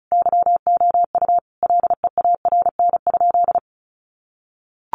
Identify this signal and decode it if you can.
Morse code, heard as a keyed tone, about 34 wpm, 700 Hz. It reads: YOU LEARN?
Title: Sentences from Top 1200 Words 35wpm